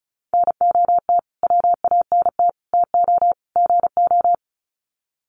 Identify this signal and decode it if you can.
Morse code, heard as a keyed tone, about 34 wpm, 700 Hz. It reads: NOT WANT TO GO
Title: Sets of 4 Words 35wpm